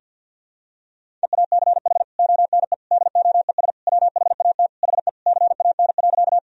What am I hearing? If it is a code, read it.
Morse code, heard as a keyed tone, about 46 wpm, 700 Hz. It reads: EACH ONE DOES WHAT HE CAN.
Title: Sentences from Top 1000 Words - Review Part 1 - 50wpm